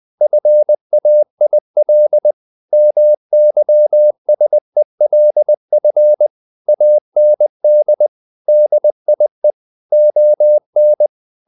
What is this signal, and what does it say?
Morse code, heard as a keyed tone, about 20 wpm, 600 Hz. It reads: FAIL MYSELF AND DIE ON
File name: the old man and the sea part 2 20wpm